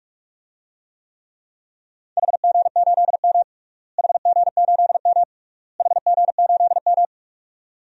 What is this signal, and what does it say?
Morse code, heard as a keyed tone, about 46 wpm, 700 Hz. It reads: HG8M HG8M HG8M
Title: DXCC-100 Call Signs - 4 Characters - Repeated 3x - 45wpm